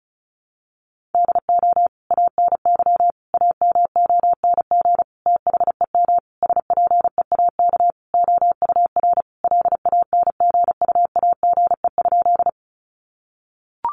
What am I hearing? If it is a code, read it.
Morse code, heard as a keyed tone, about 34 wpm, 700 Hz. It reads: DO ANY AMONG THEM SPEAK OUR LANGUAGE?
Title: Sentences from Top 1100 Words 35wpm